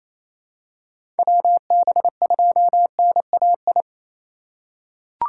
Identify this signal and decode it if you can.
Morse code, heard as a keyed tone, about 28 wpm, 700 Hz. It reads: WB2NAI
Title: USA Call Signs - 4x Character Spacing - 28wpm